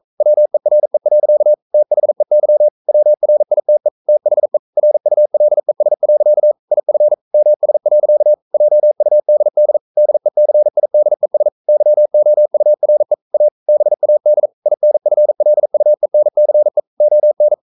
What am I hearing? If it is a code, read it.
Morse code, heard as a keyed tone, about 42 wpm, 600 Hz. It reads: WERE. THEY WRITE THE RULES. IF MS. JUDD DECIDES YOURE A BAD INFLUENCE ON
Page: Romanen Never af Ken Follet på engelsk.